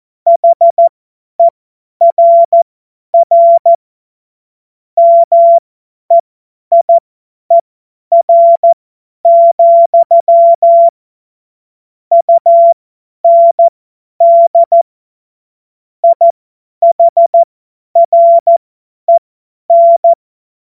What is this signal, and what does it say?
Morse code, heard as a keyed tone, about 14 wpm, spaced as for 10 wpm, 680 Hz. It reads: HERR MEIER, UND IHREN
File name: Witz-28-Oktober-2025-50-F70